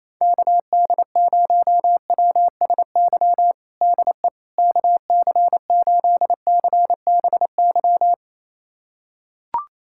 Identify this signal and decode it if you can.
Morse code, heard as a keyed tone, about 28 wpm, 700 Hz. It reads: KD0WSY DE KC8CBY
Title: USA Callsign DE Callsign 28wpm